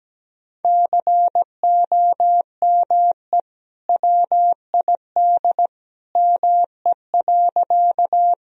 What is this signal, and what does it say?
Morse code, heard as a keyed tone, about 17 wpm, 700 Hz. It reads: COME WID ME.
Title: Sentences from Top 200 Words and Abbreviations 17wpm